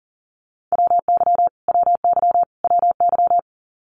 Morse code, heard as a keyed tone, about 38 wpm, 700 Hz. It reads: WY WY WY
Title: US States Repeated 3x 40wpm